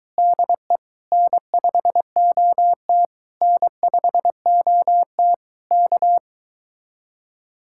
Morse code, heard as a keyed tone, about 22 wpm, 700 Hz. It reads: DE N5OT N5OT K